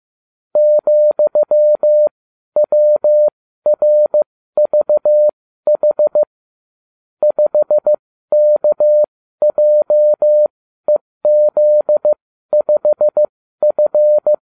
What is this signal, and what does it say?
Morse code, heard as a keyed tone, about 15 wpm, spaced as for 12 wpm, 600 Hz. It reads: ,WRVH 5KJEZ5F